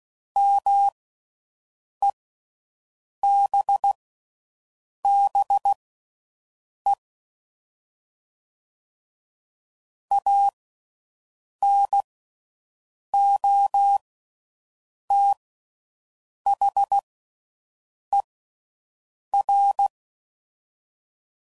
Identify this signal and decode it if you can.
Morse code, heard as a keyed tone, about 16 wpm, spaced as for 6 wpm, 800 Hz. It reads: MEBBE ANOTHER